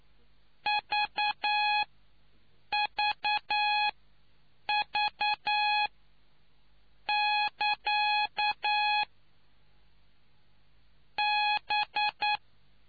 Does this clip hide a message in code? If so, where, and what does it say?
Yes: Morse code from 0.7 to 12.4 s, heard as a keyed tone, about 9 wpm, spaced as for 6 wpm, 840 Hz. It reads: VVV<KA> B